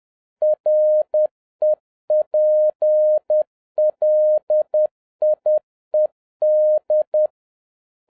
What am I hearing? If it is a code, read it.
Morse code, heard as a keyed tone, about 10 wpm, 610 Hz. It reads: REPLIED